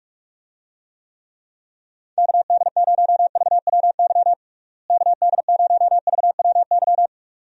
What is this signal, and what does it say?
Morse code, heard as a keyed tone, about 42 wpm, 700 Hz. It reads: KD0VWY KD0VWY
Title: USA Call Signs - 6 Characters - Repeated 2x - 45wpm